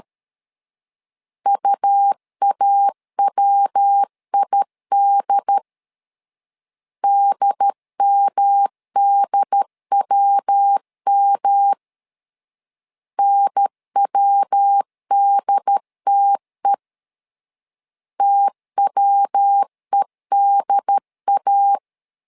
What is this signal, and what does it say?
Morse code, heard as a keyed tone, about 13 wpm, 800 Hz. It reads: UAWID DMDWM NWDTE TWEDA
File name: LEZIONI / 3 MWRO / 3 MW 12wpm 1633